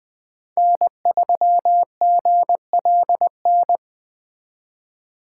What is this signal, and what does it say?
Morse code, heard as a keyed tone, about 20 wpm, 700 Hz. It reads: N3GLN
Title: Speed Racing - USA 1x3 Call Signs 17wpm